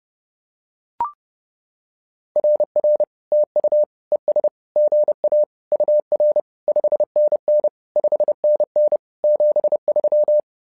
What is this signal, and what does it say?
Morse code, heard as a keyed tone, about 30 wpm, 600 Hz. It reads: RR TU ES GA UR 5NN 5NN 73
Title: SOTA QSO Practice 30wpm